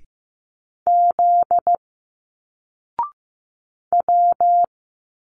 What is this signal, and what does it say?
Morse code, heard as a keyed tone, about 15 wpm, 700 Hz. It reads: Z W
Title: Single Letters 15wpm